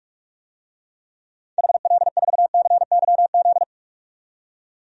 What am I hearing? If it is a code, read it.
Morse code, heard as a keyed tone, about 46 wpm, 700 Hz. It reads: HL4CYZ
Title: DXCC-100 Call Signs - 6 Characters - 45wpm